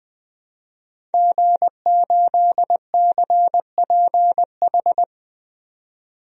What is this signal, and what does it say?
Morse code, heard as a keyed tone, about 20 wpm, 700 Hz. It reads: G8CPH
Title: DXCC-100 Call Signs - 5 Characters - 20wpm